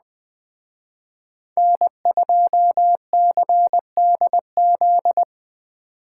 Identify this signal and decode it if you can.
Morse code, heard as a keyed tone, about 21 wpm, 700 Hz. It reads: N2CDZ
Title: Speed Racing - USA 1x3 Call Signs 17wpm